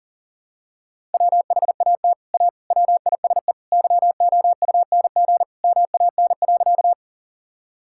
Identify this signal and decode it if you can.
Morse code, heard as a keyed tone, about 40 wpm, 700 Hz. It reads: WHAT A WISE YOUNG MAN.
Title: Sentences from Top 1600 Words 40wpm